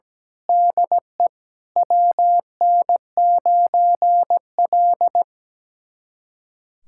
Morse code, heard as a keyed tone, about 17 wpm, 700 Hz. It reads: DE WN9L
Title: USA Callsign DE Callsign 17wpm